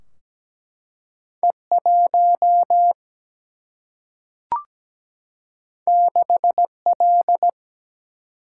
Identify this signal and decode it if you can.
Morse code, heard as a keyed tone, about 17 wpm, 700 Hz. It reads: E1 6L
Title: Single-Letter Single-Number 17wpm